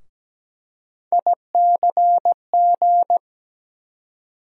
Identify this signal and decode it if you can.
Morse code, heard as a keyed tone, about 17 wpm, 700 Hz. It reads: ICG
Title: USA Call Sign Postfixes - 3 Letters 17wpm